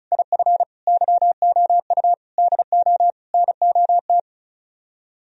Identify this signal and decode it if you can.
Morse code, heard as a keyed tone, about 42 wpm, 700 Hz. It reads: IF YOU DO NOT
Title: Speed-Racing - Sets of 5 Words 35wpm